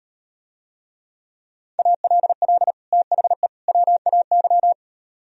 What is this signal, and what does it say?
Morse code, heard as a keyed tone, about 38 wpm, 700 Hz. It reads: ALL THE WAY